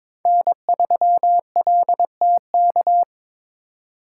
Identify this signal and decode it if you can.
Morse code, heard as a keyed tone, about 26 wpm, 700 Hz. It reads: N3LTK
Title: Speed Racing - USA 1x3 Call Signs 22wpm